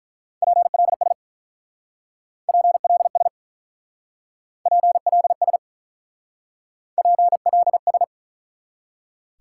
Morse code, heard as a keyed tone, about 44 wpm, 700 Hz. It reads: PLS PLS PLS PLS